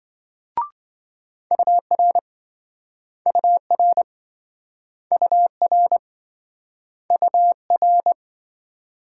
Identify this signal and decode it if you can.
Morse code, heard as a keyed tone, about 26 wpm, 700 Hz. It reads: UR UR UR UR